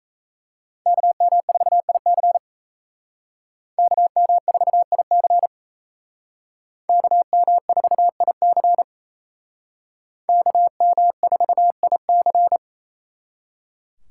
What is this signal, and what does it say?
Morse code, heard as a keyed tone, about 32 wpm, 700 Hz. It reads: KM4IC KM4IC KM4IC KM4IC
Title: Speed Racing - USA Call Signs 28wpm